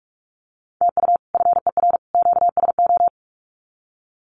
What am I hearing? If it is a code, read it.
Morse code, heard as a keyed tone, about 44 wpm, 700 Hz. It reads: TU FER QSO
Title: Common QSO Phrases 45wpm